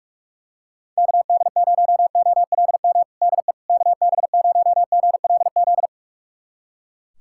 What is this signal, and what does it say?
Morse code, heard as a keyed tone, about 44 wpm, 700 Hz. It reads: KD0OLM DE KB0GLZ
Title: USA Callsign DE Callsign - 6 Characters - 45wpm